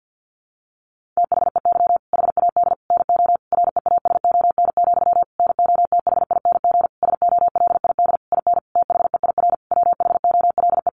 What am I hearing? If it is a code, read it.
Morse code, heard as a keyed tone, about 50 wpm, 700 Hz. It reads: THEY HAD NO REASON, NOTHING SOLID IN THEIR WHOLE
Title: Sentences from Top 1500 Words 50wpm